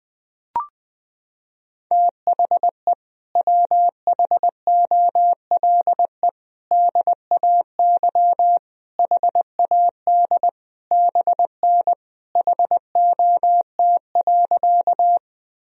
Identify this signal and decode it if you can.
Morse code, heard as a keyed tone, about 21 wpm, 700 Hz. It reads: THE WHOLE DAY HAD BN HOT.
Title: Sentences from Top 500 Words and Abbreviations 20wpm